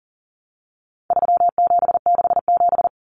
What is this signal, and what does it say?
Morse code, heard as a keyed tone, about 40 wpm, 700 Hz. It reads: 3767
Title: Numbers - 4-Digit 40wpm